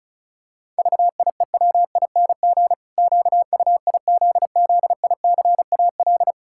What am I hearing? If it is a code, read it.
Morse code, heard as a keyed tone, about 36 wpm, 700 Hz. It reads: VIEWING QUIZZICAL
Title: Pangram Challenge v2 - 35wpm